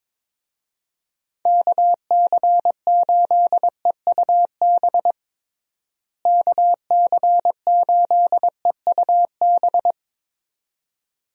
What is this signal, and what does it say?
Morse code, heard as a keyed tone, about 22 wpm, 700 Hz. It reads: KC8EUB KC8EUB
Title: USA Call Signs - 6 Characters - Repeated 2x - 22wpm